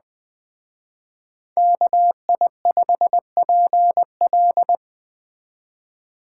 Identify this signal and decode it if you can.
Morse code, heard as a keyed tone, about 20 wpm, 700 Hz. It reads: KI5PL